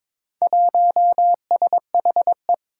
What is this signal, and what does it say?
Morse code, heard as a keyed tone, about 21 wpm, 700 Hz. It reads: 1SHE
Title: Four Challenging Letter-Number Combinations 22wpm